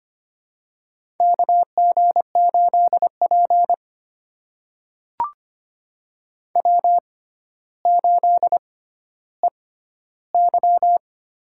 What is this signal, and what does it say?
Morse code, heard as a keyed tone, about 25 wpm, spaced as for 13 wpm, 700 Hz. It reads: KG8P W8EY
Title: USA Call Signs - 4 Characters - 6x Character Spacing - 25wpm